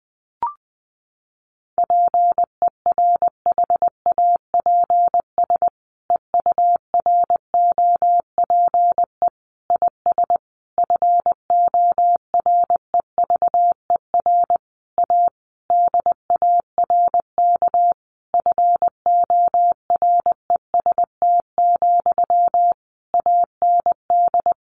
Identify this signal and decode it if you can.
Morse code, heard as a keyed tone, about 20 wpm, 700 Hz. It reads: PERHAPS EUROPE IS FOREVER A DARK FOREST, AND
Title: A Sound Of Thunder 20wpm